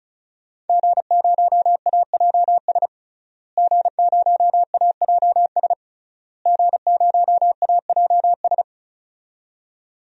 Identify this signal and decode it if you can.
Morse code, heard as a keyed tone, about 36 wpm, 700 Hz. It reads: G0AJS G0AJS G0AJS
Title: DXCC-100 Call Signs - 5 Characters - Repeated 3x - 35wpm